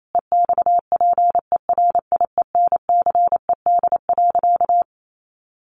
Morse code, heard as a keyed tone, about 28 wpm, 700 Hz. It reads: EXPERIENCED.
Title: Sentences from Top 1100 Words 28wpm